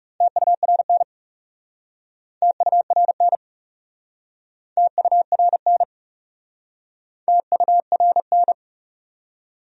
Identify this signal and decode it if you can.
Morse code, heard as a keyed tone, about 40 wpm, 700 Hz. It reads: TURN TURN TURN TURN